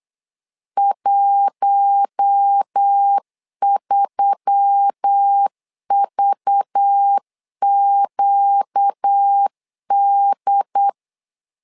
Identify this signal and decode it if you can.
Morse code, heard as a keyed tone, about 8 wpm, 800 Hz. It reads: 13VQD
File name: LEZIONI / 5 LBVC / 5 LBV 8wpm 2837